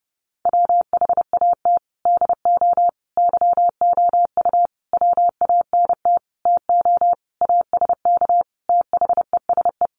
Morse code, heard as a keyed tone, about 28 wpm, 700 Hz. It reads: WHAT DO YOU WANT TO ASK THESE
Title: Sentences from Top 200 Words 30wpm